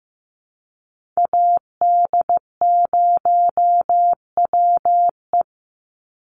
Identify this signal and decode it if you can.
Morse code, heard as a keyed tone, about 15 wpm, 700 Hz. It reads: AD0WE
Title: Call Signs Repeated 2x - 15wpm